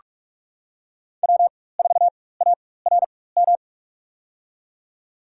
Morse code, heard as a keyed tone, about 46 wpm, spaced as for 21 wpm, 700 Hz. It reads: W4ARM
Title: USA Call Signs - 5 Characters - 4x Character Spacing - 45wpm